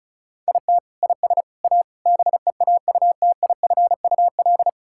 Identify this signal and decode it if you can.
Morse code, heard as a keyed tone, about 34 wpm, 690 Hz. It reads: IT IS A BEAUTIFUL
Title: Sentences from Top 1000 Words - Review Part 6 - 35wpm